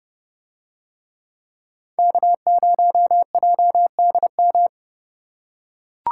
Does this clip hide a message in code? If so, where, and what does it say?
Yes: Morse code from 2.0 to 4.7 s, heard as a keyed tone, about 29 wpm, 700 Hz. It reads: K0JDM